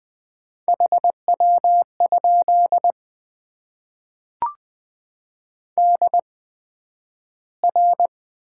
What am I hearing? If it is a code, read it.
Morse code, heard as a keyed tone, about 20 wpm, spaced as for 11 wpm, 700 Hz. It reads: HW? DR
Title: Top 90 QSO Elements - 8x Character Spacing - 20wpm